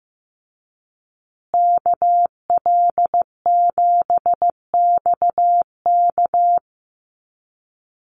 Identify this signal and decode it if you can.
Morse code, heard as a keyed tone, about 15 wpm, 700 Hz. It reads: KL7XK
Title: USA Call Signs Repeated 2x 15wpm